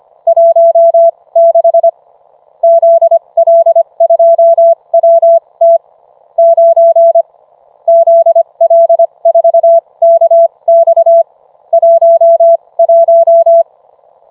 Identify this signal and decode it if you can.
Morse code, heard as a keyed tone, about 25 wpm, spaced as for 21 wpm, 670 Hz. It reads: 16 ZL2WT 9 ZL4KX 11